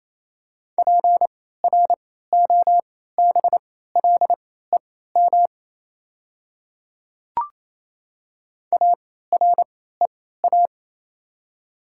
Morse code, heard as a keyed tone, about 28 wpm, spaced as for 16 wpm, 690 Hz. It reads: PROBLEM AREA
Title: 3x Character Spacing - Top 500 Words - 28wpm